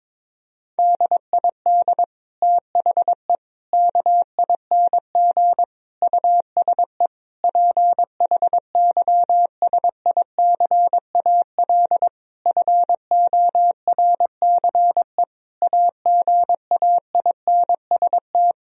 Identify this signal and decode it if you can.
Morse code, heard as a keyed tone, about 22 wpm, 700 Hz. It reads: DID THE KING USE PHYSICAL FORCE AGAINST
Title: Sentences from Top 1000 Words - Review Part 4 - 22wpm